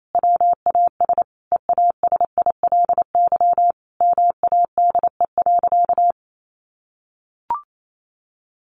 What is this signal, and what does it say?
Morse code, heard as a keyed tone, about 28 wpm, 700 Hz. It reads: WAS EASILY MADE.
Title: Sentences from Top 1200 Words 28wpm